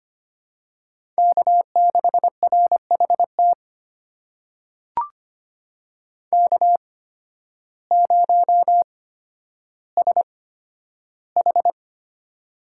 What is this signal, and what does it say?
Morse code, heard as a keyed tone, about 25 wpm, spaced as for 12 wpm, 700 Hz. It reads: K6RHT K0SH
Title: USA Call Signs - 8x Character Spacing - 25wpm